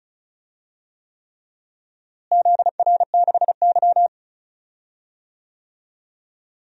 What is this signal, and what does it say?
Morse code, heard as a keyed tone, about 36 wpm, 700 Hz. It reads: ZR6Y
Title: DXCC-100 Call Signs - 4 Characters - Repeated 3x - 35wpm